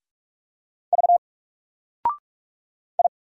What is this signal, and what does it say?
Morse code, heard as a keyed tone, about 44 wpm, 700 Hz. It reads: V I
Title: Single Character (Letter:Number:Punctuation:Prosign) 45wpm